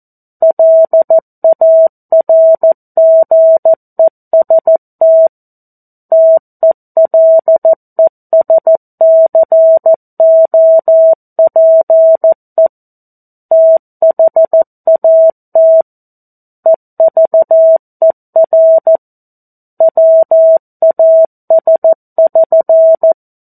Morse code, heard as a keyed tone, about 14 wpm, 650 Hz. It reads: LARGEST TELESCOPE THAT EVER WAS<SN>
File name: 02-Chapter-02 14 WPM